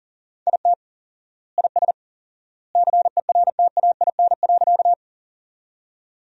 Morse code, heard as a keyed tone, about 40 wpm, 700 Hz. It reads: IT IS CERTAIN.
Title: 4x Word-Spacing - Sentences from Top 500 Words - 40wpm